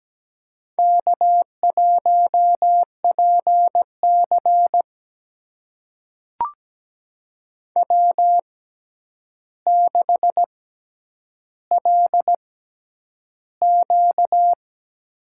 Morse code, heard as a keyed tone, about 17 wpm, spaced as for 9 wpm, 700 Hz. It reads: K1PC W6LQ
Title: USA Call Signs - 6x Character Spacing - 17wpm